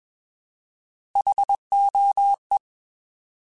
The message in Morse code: HOE

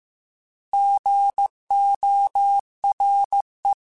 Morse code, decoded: GORE

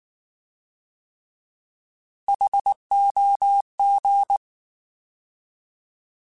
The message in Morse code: HOG

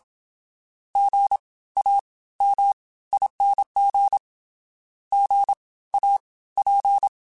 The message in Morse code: GAMING GAP

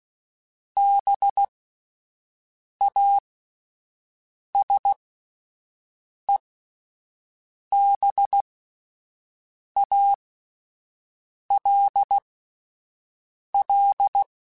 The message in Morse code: BASEBALL

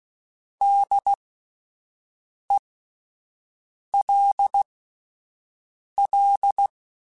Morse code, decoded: DELL

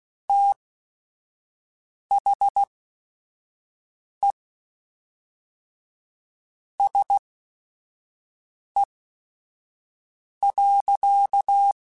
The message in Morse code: THE SE.

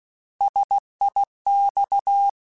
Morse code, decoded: SIX

